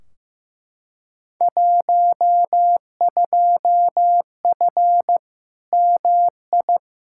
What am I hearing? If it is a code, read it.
Morse code, heard as a keyed tone, about 16 wpm, 700 Hz. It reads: 12F MI
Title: Field Day Exchanges Encore - 15wpm